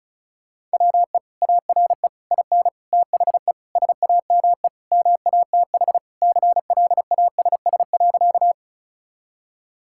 Morse code, heard as a keyed tone, about 36 wpm, 700 Hz. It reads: WE ARE IN THE SAME MATH CLASS.
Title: Sentences from Top 1000 Words - Review Part 6 - 35wpm